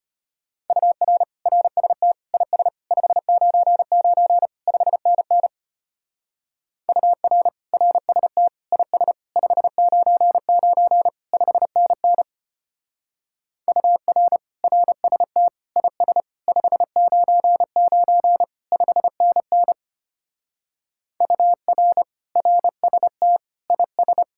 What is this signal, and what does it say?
Morse code, heard as a keyed tone, about 30 wpm, 700 Hz. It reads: UR RST IS 599 5NN UR RST IS 599 5NN UR RST IS 599 5NN UR RST IS
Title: Speed-Racing Common QSO Phrases - 25wpm